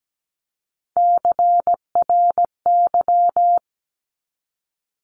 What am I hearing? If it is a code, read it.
Morse code, heard as a keyed tone, about 17 wpm, 700 Hz. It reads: CRY
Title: Top 500 Words 17wpm